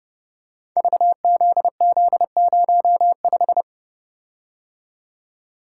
Morse code, heard as a keyed tone, about 30 wpm, 700 Hz. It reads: VZZ05
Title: Code Groups - 5 Letter:Numbers - 30wpm